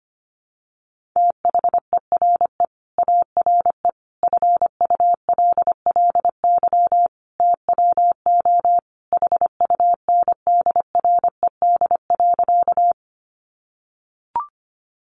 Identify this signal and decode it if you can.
Morse code, heard as a keyed tone, about 25 wpm, 700 Hz. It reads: THERE ARE FULLY TWO HUNDRED.
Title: Sentences from Top 1300 Words 25wpm